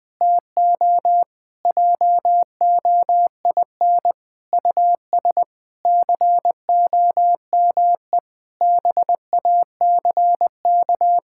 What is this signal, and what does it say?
Morse code, heard as a keyed tone, about 20 wpm, 700 Hz. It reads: TO JOIN US COME BACK
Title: Sentences with 4-Word Phrases - Spoken Beginning - 20wpm